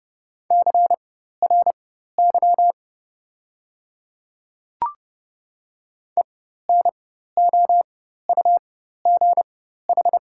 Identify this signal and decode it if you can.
Morse code, heard as a keyed tone, about 30 wpm, spaced as for 14 wpm, 690 Hz. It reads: CRY ENOUGH
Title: 4x Character Spacing - Top 500 Words - 30wpm